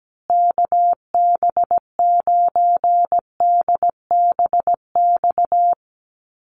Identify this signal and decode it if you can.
Morse code, heard as a keyed tone, about 17 wpm, 700 Hz. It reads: KB9DBX
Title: USA Call Signs Repeated 3x 17wpm